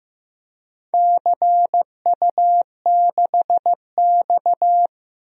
Morse code, heard as a keyed tone, about 15 wpm, 700 Hz. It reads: CU6X
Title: DXCC-100 Call Signs - 4 Characters - 15wpm